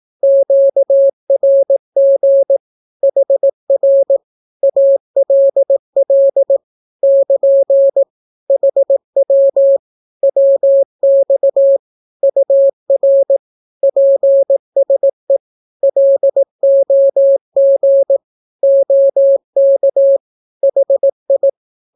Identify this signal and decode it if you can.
Morse code, heard as a keyed tone, about 18 wpm, 550 Hz. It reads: QRG HR ALL <KN> HW WX UR PSE LOG OK HI
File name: naukapodstaw / skroty / skroty